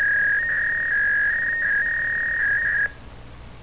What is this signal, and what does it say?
Fax machine handshake.